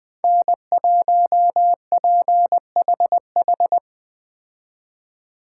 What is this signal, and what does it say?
Morse code, heard as a keyed tone, about 21 wpm, 700 Hz. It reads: N1PHH